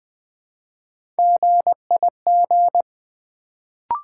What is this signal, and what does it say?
Morse code, heard as a keyed tone, about 20 wpm, 700 Hz. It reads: GIG
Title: Build A Word - 1 Letter At A Time - 3-Letter Words - 20wpm